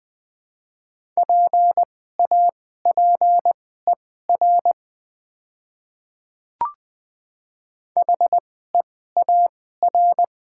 Morse code, heard as a keyed tone, about 20 wpm, spaced as for 14 wpm, 700 Hz. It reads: PAPER HEAR
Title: 2x Character Spacing - Top 300 Words - 20wpm